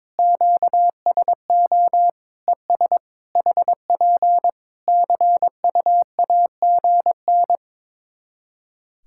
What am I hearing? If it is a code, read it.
Morse code, heard as a keyed tone, about 22 wpm, 700 Hz. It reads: QSO ES HP CUAGN
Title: Practice QSO - Exchange 4 - Ending - 22wpm